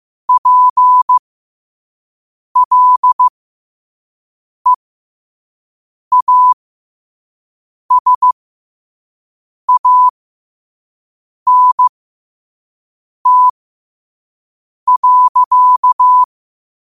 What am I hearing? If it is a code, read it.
Morse code, heard as a keyed tone, about 15 wpm, spaced as for 5 wpm, 1000 Hz. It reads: PLEASANT.
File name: qotdfiles / 5wpm